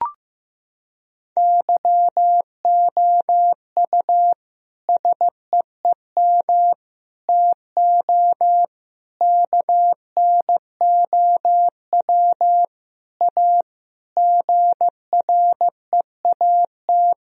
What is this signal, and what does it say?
Morse code, heard as a keyed tone, about 16 wpm, 700 Hz. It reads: YOU SEEM TO KNOW A GREAT
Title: Sentences from Top 1000 Words - Review Part 1 - 15wpm